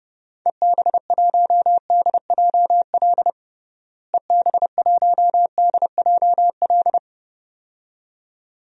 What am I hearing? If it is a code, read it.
Morse code, heard as a keyed tone, about 30 wpm, 700 Hz. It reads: EB1DJL EB1DJL
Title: DXCC-100 Call Signs - 6 Characters - Repeated 2x - 30wpm